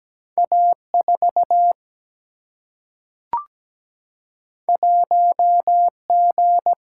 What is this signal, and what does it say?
Morse code, heard as a keyed tone, about 17 wpm, 700 Hz. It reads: A4 1G
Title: Single-Letter Single-Number 17wpm